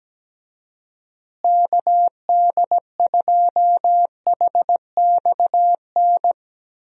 Morse code, heard as a keyed tone, about 17 wpm, 700 Hz. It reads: KD2HXN
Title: USA Call Signs - 6 Characters - Repeated 3x - 17wpm